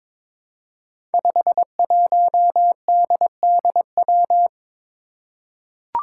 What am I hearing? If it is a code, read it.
Morse code, heard as a keyed tone, about 22 wpm, 700 Hz. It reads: 51DDW
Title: Code Groups - 5 Letter:Numbers - 22wpm